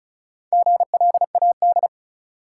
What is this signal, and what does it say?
Morse code, heard as a keyed tone, about 36 wpm, 700 Hz. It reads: GLAD
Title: Sentences from Top 800 Words - New Word Drill - 35wpm